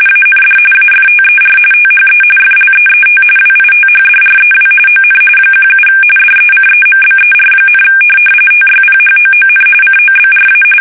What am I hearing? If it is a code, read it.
RTTY, about 45.45 baud. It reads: AND AFTER THE FLEET WITHDREW ON 17 DECEMBER, TYPHOON COBRA
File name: fsk850 / USS-Massachusetts